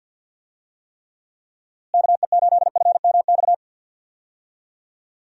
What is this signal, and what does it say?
Morse code, heard as a keyed tone, about 50 wpm, 700 Hz. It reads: KE8FMX
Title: USA Call Signs - 6 Characters - Repeated 3x - 50wpm